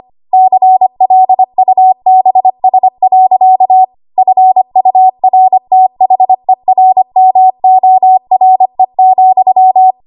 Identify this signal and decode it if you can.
Morse code, heard as a keyed tone, about 25 wpm, 750 Hz. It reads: CLUBS. FURTHERMORE,
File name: Archive / 25 WPM / 25WPM